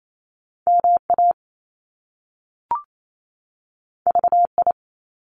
Morse code, heard as a keyed tone, about 28 wpm, 700 Hz. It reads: MA VI
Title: Top Two Letter Combinations in Words 28wpm